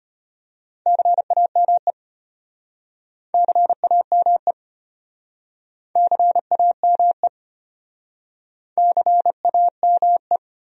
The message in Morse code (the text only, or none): CAME CAME CAME CAME